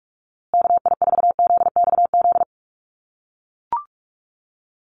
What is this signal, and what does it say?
Morse code, heard as a keyed tone, about 44 wpm, 700 Hz. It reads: KI4ZXZ
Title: USA Call Signs Repeated 2x 45wpm